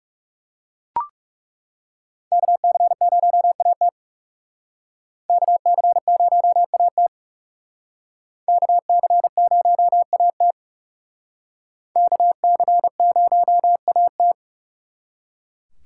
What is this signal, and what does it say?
Morse code, heard as a keyed tone, about 37 wpm, 700 Hz. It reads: KC0AT KC0AT KC0AT KC0AT
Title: Speed Racing - USA 2x2 Call Signs 30wpm